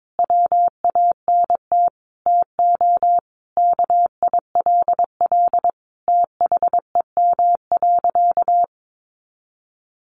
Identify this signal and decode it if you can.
Morse code, heard as a keyed tone, about 22 wpm, 700 Hz. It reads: WANT TO KILL THEM.
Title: Additional Easy Phrases 22wpm